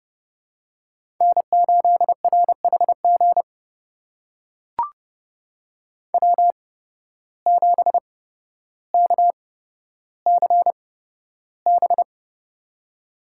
Morse code, heard as a keyed tone, about 32 wpm, spaced as for 13 wpm, 700 Hz. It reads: N8RHG W7KCB
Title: USA Call Signs - 8x Character Spacing - 30wpm